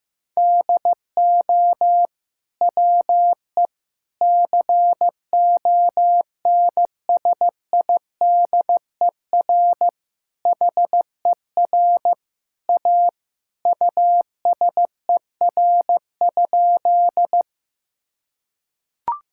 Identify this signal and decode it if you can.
Morse code, heard as a keyed tone, about 16 wpm, 700 Hz. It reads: DO WE CONSIDER HER A USER?
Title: Sentences from Top 1000 Words - Review Part 5 - 15wpm